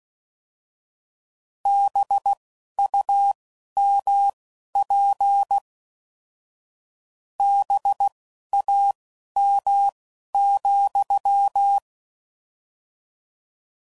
Morse code, heard as a keyed tone, about 16 wpm, spaced as for 11 wpm, 800 Hz. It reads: BUMP BAM,